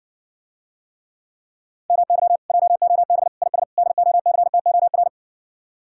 Morse code, heard as a keyed tone, about 60 wpm, 700 Hz. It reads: MY JOB IS DOCTOR